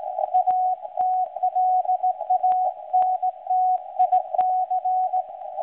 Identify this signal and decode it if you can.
Morse code, heard as a keyed tone, about 17 wpm, 740 Hz. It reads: ATLANTIC
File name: WebSDR records / SAQ recording start 17.2kHz